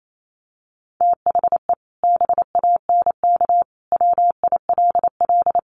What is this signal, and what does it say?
Morse code, heard as a keyed tone, about 28 wpm, 700 Hz. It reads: THE BANK WILL
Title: Sentences from Top 1500 Words 28wpm